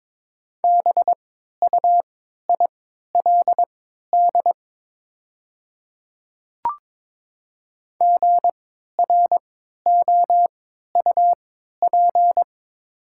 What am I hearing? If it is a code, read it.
Morse code, heard as a keyed tone, about 22 wpm, spaced as for 12 wpm, 690 Hz. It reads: BUILD GROUP
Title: 3x Character Spacing - Top 500 Words - 22wpm